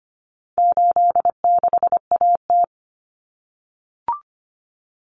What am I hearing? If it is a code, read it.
Morse code, heard as a keyed tone, about 25 wpm, 700 Hz. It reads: 86AT
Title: Common Cut Numbers - 4-Digit 25wpm